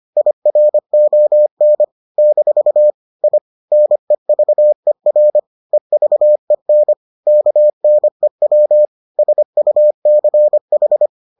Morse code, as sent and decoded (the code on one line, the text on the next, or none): .. .-. --- -. / -....- / .. / -. ...- . .-. / . ...- . -. / -.- -. .-- / ... ..- -.-. ....
IRON - I NEVER EVEN KNEW SUCH